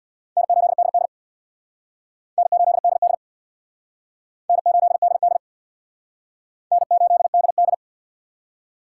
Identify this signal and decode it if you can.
Morse code, heard as a keyed tone, about 60 wpm, 700 Hz. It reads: N8DD N8DD N8DD N8DD
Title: Speed Racing - USA Call Signs 50wpm